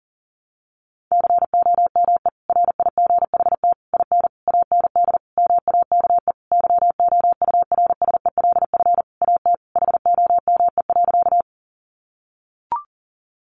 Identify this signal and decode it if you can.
Morse code, heard as a keyed tone, about 40 wpm, 700 Hz. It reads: COME RIGHT IN AND MAKE YOURSELF AT HOME.
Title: Sentences from Top 600 Words 40wpm